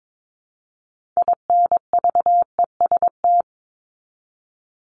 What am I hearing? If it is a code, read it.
Morse code, heard as a keyed tone, about 22 wpm, 700 Hz. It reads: INVEST
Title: Top 1001-1250 Words 22wpm